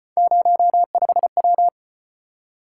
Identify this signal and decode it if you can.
Morse code, heard as a keyed tone, about 35 wpm, 700 Hz. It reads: 05W